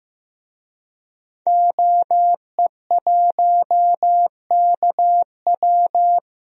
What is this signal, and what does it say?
Morse code, heard as a keyed tone, about 15 wpm, 700 Hz. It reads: OE1KW
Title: DXCC-100 Call Signs - 5 Characters - 15wpm